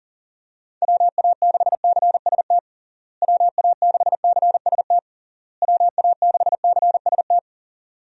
Morse code, heard as a keyed tone, about 40 wpm, 700 Hz. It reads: WA6CST WA6CST WA6CST
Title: USA Call Signs - 6 Characters - Repeated 3x - 40wpm